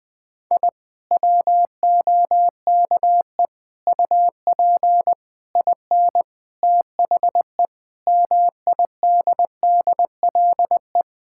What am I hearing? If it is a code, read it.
Morse code, heard as a keyed tone, about 20 wpm, 700 Hz. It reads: I WOKE UP IN THE MIDDLE
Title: Sentences with 7-Word Phrases - 20wpm